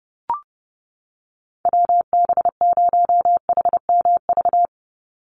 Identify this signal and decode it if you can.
Morse code, heard as a keyed tone, about 29 wpm, 700 Hz. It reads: WB0HMV
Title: USA Call Signs 30wpm